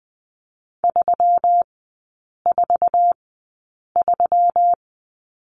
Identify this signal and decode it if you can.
Morse code, heard as a keyed tone, about 20 wpm, spaced as for 8 wpm, 700 Hz. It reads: 343
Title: 138 Lesson 23e - Letter f - Three Characters 20wpm